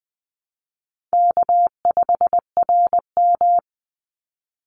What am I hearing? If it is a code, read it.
Morse code, heard as a keyed tone, about 20 wpm, 700 Hz. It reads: K5RM
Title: Easy Call Signs Repeated 3x - 20wpm